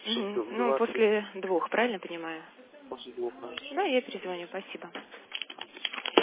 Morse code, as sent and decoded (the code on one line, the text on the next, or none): none